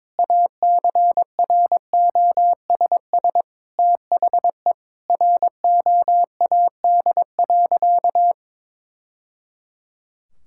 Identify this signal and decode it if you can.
Morse code, heard as a keyed tone, about 22 wpm, 700 Hz. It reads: ACROSS THE ROAD.
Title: Sentences from Top 1000 Words - Review Part 5 - 22wpm